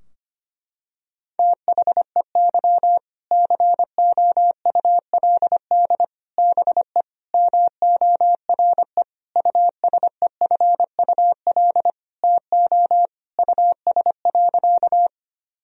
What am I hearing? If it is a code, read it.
Morse code, heard as a keyed tone, about 25 wpm, 700 Hz. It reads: THEY COULD BE MORE USEFUL TO US.
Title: Sentences from Top 1000 Words - Review Part 4 - 25wpm